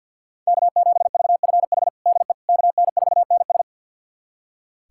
Morse code, heard as a keyed tone, about 45 wpm, 700 Hz. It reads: K7VFH DE KN4NS
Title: USA Callsign DE Callsign - 5 Characters - 50wpm